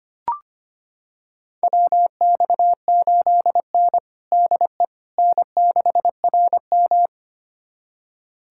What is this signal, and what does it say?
Morse code, heard as a keyed tone, about 25 wpm, 700 Hz. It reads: WX8N DE N6RM
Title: USA Callsign DE Callsign - 4 Characters - 25wpm